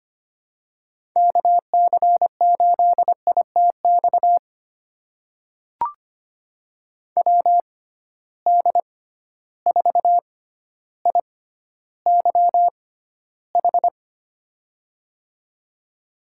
Morse code, heard as a keyed tone, about 25 wpm, spaced as for 13 wpm, 700 Hz. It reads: KC8ITX WD4IYH
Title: USA Call Signs - 6 Characters - 6x Character Spacing - 25wpm